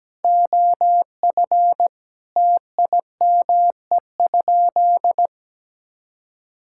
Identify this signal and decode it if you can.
Morse code, heard as a keyed tone, about 17 wpm, 700 Hz. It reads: OF TIME?
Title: Sentences from Top 1000 Words - Review Part 5 - 17wpm